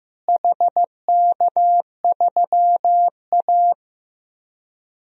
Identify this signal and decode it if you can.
Morse code, heard as a keyed tone, about 15 wpm, 700 Hz. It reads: HK3A
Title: DXCC-100 Call Signs - 4 Characters - Repeated 3x - 15wpm